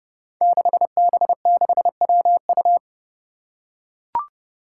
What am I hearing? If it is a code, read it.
Morse code, heard as a keyed tone, about 30 wpm, 700 Hz. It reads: 6B6WU
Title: Five Challenging Letter-Number Combinations 30wpm